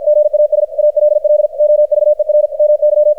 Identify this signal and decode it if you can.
Morse code, heard as a keyed tone, about 53 wpm, 600 Hz. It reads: GAS ALL OVER MY
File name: archived / CRSnet